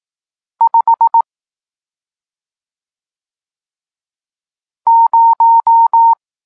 Morse code, heard as a keyed tone, about 18 wpm, 920 Hz. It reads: 5 0
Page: Tekst 40 – Nauka znaków przestankowych